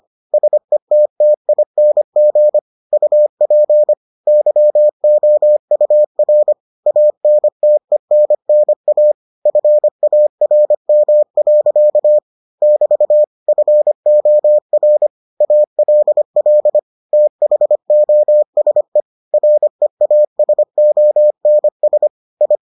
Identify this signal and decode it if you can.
Morse code, heard as a keyed tone, about 25 wpm, 600 Hz. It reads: SETTING UP YOUR ANTENNA FARM. <BT> FOR ALL THOSE REASONS I